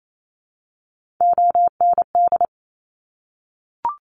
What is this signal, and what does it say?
Morse code, heard as a keyed tone, about 28 wpm, 700 Hz. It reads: OND
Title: Top Three Letter Combinations in Words 28wpm